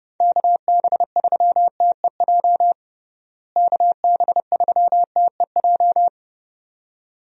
Morse code, heard as a keyed tone, about 30 wpm, 700 Hz. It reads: KB3TEJ KB3TEJ
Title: USA Call Signs - 6 Characters - Repeated 2x - 30wpm